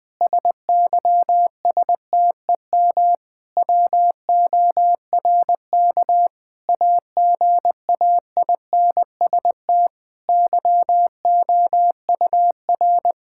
Morse code, heard as a keyed tone, about 20 wpm, 700 Hz. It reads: SYSTEM WORK AGAINST YOUR
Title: Sentences from Top 1000 Words - Review Part 5 - 20wpm